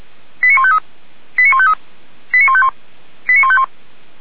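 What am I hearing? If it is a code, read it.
Five-tone selective call (ZVEI), heard as quick runs of five tones, tones of about 70 ms. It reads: 89153 89153 89151 89151
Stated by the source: ZVEI five-tone call received on SDR by Feuerwehr Orlamünde.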